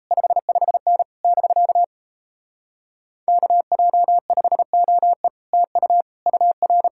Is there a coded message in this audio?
Yes: Morse code from 0.1 to 6.9 s, heard as a keyed tone, about 38 wpm, 700 Hz. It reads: 55N <BK> KJ5OE TU UR